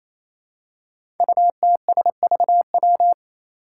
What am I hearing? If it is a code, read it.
Morse code, heard as a keyed tone, about 27 wpm, 700 Hz. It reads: UTSVW
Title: Five Challenging Letter-Number Combinations 28wpm